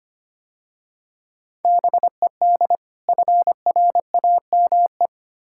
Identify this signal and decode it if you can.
Morse code, heard as a keyed tone, about 29 wpm, 700 Hz. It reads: BED FRAME
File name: Easy Words - Speed-Racing with Hints - 25wpm